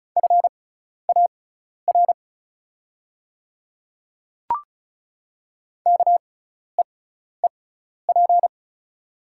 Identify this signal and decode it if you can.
Morse code, heard as a keyed tone, about 34 wpm, spaced as for 12 wpm, 690 Hz. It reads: FAR KEEP
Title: 6x Character Spacing - Top 500 Words - 35wpm